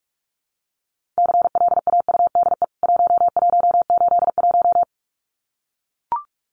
Morse code, heard as a keyed tone, about 44 wpm, 700 Hz. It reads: CLAUDE 1181
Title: CWT Exchange 45wpm